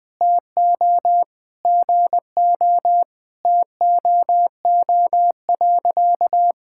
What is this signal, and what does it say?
Morse code, heard as a keyed tone, about 21 wpm, 700 Hz. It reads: TO GO TOO.
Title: Sentences with 3-Word Phrases - 20wpm